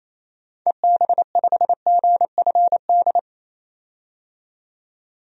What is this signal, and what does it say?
Morse code, heard as a keyed tone, about 28 wpm, 700 Hz. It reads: EB5GFD
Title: DXCC-100 Call Signs - 6 Characters - 28wpm